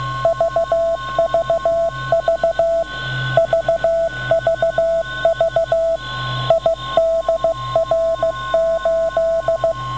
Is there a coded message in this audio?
Yes: Morse code from 0.2 to 9.7 s, heard as a keyed tone, about 15 wpm, 660 Hz. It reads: VVV VVV IDR8